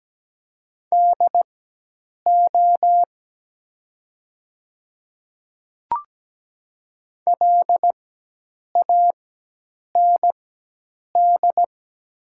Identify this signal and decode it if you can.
Morse code, heard as a keyed tone, about 17 wpm, spaced as for 8 wpm, 700 Hz. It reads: DO LAND
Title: 4x Character Spacing - Top 200 Words - 17wpm